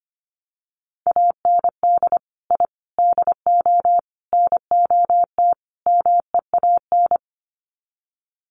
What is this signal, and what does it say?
Morse code, heard as a keyed tone, about 25 wpm, 700 Hz. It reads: AND I DO NOT MEAN
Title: Sets of 5 Words Encore2 25wpm